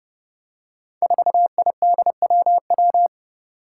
Morse code, heard as a keyed tone, about 30 wpm, 700 Hz. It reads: 4IDWW
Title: Five Challenging Letter-Number Combinations 30wpm